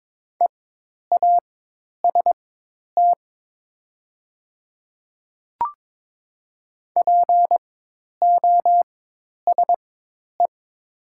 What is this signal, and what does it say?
Morse code, heard as a keyed tone, about 22 wpm, spaced as for 10 wpm, 700 Hz. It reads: EAST POSE
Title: 4x Character Spacing - Top 500 Words - 22wpm